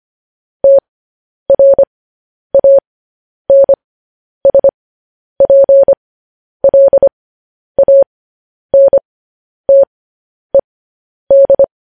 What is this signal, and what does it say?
Morse code, heard as a keyed tone, about 25 wpm, spaced as for 10 wpm, 550 Hz. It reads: TRANSPLANTED
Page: news headlines in morse code at 10 wpm